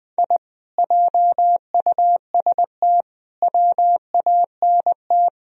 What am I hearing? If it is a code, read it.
Morse code, heard as a keyed tone, about 22 wpm, 700 Hz. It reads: I JUST WANT
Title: Speed-Racing - Sets of 4 Words 20wpm